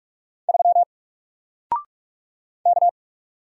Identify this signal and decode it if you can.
Morse code, heard as a keyed tone, about 44 wpm, 700 Hz. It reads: 3 K
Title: Single Letter-Number Alt 45wpm